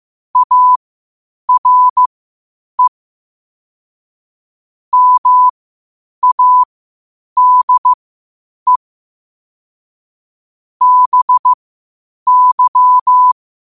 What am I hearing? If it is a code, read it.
Morse code, heard as a keyed tone, about 15 wpm, spaced as for 8 wpm, 1000 Hz. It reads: ARE MADE BY
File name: qotdfiles / 8wpm